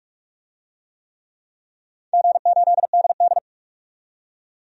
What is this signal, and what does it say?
Morse code, heard as a keyed tone, about 46 wpm, 700 Hz. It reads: G8DD
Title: DXCC-100 Call Signs - 4 Characters - 45wpm